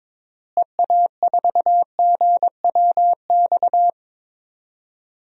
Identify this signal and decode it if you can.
Morse code, heard as a keyed tone, about 22 wpm, 700 Hz. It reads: EA4GWX
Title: DXCC-100 Call Signs - 6 Characters - Repeated 2x - 22wpm